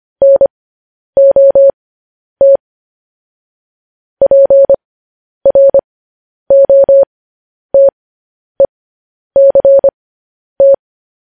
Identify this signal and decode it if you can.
Morse code, heard as a keyed tone, about 25 wpm, spaced as for 10 wpm, 550 Hz. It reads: NOT PROTECT